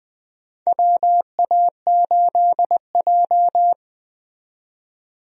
Morse code, heard as a keyed tone, about 20 wpm, 700 Hz. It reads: WA8J
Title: USA Call Signs - 4 Characters - 20wpm